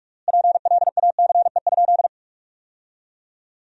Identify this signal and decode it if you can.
Morse code, heard as a keyed tone, about 47 wpm, 700 Hz. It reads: PLACE?
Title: 3x Word-Spacing - Sentences from Top 200 Words - 45wpm